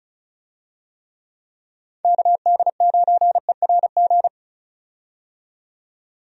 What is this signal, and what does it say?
Morse code, heard as a keyed tone, about 36 wpm, 700 Hz. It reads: KD9ERG
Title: USA Call Signs - 6 Characters - 35wpm